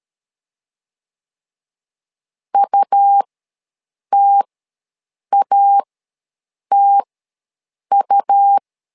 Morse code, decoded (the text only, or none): UTATU